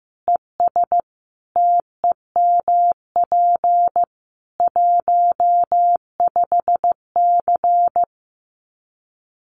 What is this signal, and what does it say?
Morse code, heard as a keyed tone, about 15 wpm, 700 Hz. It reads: ES TEMP 15C
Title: Common QSO Phrases 15wpm